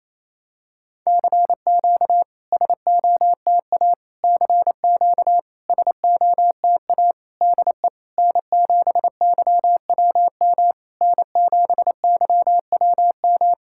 Morse code, heard as a keyed tone, about 28 wpm, 700 Hz. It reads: CQ SOTA CQ SOTA DE N7YWM N7YWM
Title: SOTA QSO Practice 28wpm